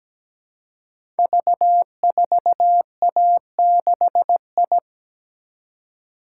Morse code, heard as a keyed tone, about 17 wpm, 700 Hz. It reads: V4A6I
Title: Code Groups - 5 Letter:Numbers - 17wpm